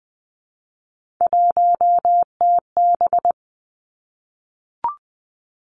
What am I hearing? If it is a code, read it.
Morse code, heard as a keyed tone, about 20 wpm, 700 Hz. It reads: 1TB
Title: Three Challenging Letter-Number Combinations 20wpm